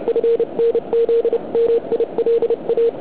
Morse code, heard as a keyed tone, about 30 wpm, 460 Hz. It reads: FNZ MILA
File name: ARCHIV / QRP2008 / OK1IC